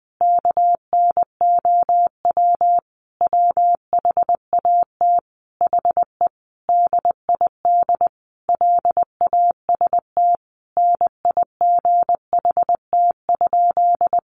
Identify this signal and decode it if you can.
Morse code, heard as a keyed tone, about 20 wpm, 700 Hz. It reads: KNOW WHAT HE DID LAST NIGHT?
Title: Sentences from Top 300 Words 20wpm